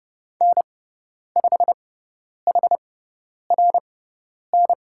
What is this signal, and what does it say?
Morse code, heard as a keyed tone, about 30 wpm, spaced as for 10 wpm, 700 Hz. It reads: N5HRN
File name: Lesson 17g - Letter u - Call Signs 30wpm